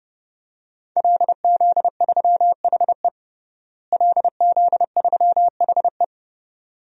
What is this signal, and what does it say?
Morse code, heard as a keyed tone, about 32 wpm, 700 Hz. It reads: LZ3HE LZ3HE
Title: DXCC-100 Call Signs - 5 Characters - Repeated 2x - 30wpm